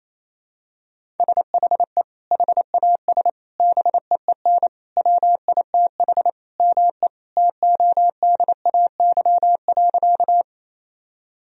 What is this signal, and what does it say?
Morse code, heard as a keyed tone, about 28 wpm, 700 Hz. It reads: SHE HAS BEEN WITH ME TODAY.
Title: Sentences from Top 1000 Words - Review Part 4 - 28wpm